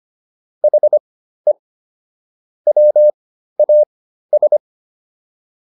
Morse code, heard as a keyed tone, about 25 wpm, spaced as for 13 wpm, 600 Hz. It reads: HE WAS